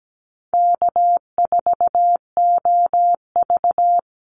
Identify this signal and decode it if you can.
Morse code, heard as a keyed tone, about 17 wpm, 700 Hz. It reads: K4OV
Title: Easy Call Signs - 17wpm